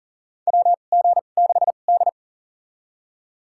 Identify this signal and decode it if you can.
Morse code, heard as a keyed tone, about 40 wpm, spaced as for 29 wpm, 700 Hz. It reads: WG6D
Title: USA Call Signs - 4 Characters - 2x Character Spacing - 40wpm